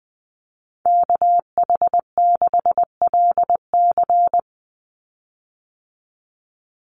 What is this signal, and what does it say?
Morse code, heard as a keyed tone, about 20 wpm, 700 Hz. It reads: KH6LC
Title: Call Signs Repeated 3x - 20wpm